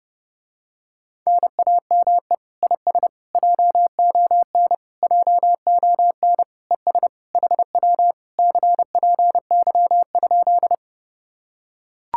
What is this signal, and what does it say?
Morse code, heard as a keyed tone, about 29 wpm, 700 Hz. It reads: NAME IS JON JON ES HW CPY?